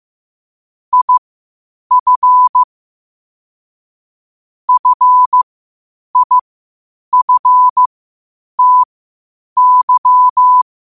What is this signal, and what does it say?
Morse code, heard as a keyed tone, about 15 wpm, spaced as for 8 wpm, 1000 Hz. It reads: IF FIFTY
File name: qotdfiles / 8wpm